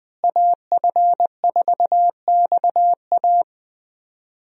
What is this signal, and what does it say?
Morse code, heard as a keyed tone, about 20 wpm, 700 Hz. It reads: AF4XA
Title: USA Call Signs - 5 Characters - Repeated 2x - 20wpm